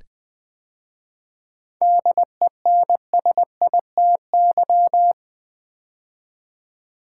Morse code, heard as a keyed tone, about 20 wpm, 690 Hz. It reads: DENSITY
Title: 7 Letter Words - First 3 Letters Given - 20wpm